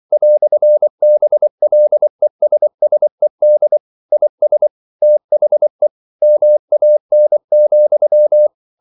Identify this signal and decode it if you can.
Morse code, heard as a keyed tone, about 24 wpm, 600 Hz. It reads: "BLESSED IS THE MAN,
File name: CW / Practice / EnglishQuotations0224